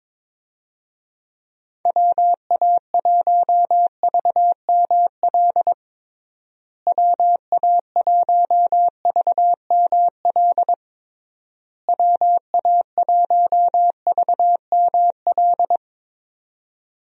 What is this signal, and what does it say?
Morse code, heard as a keyed tone, about 22 wpm, 700 Hz. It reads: WA1VML WA1VML WA1VML
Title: USA Call Signs - 6 Characters - Repeated 3x - 22wpm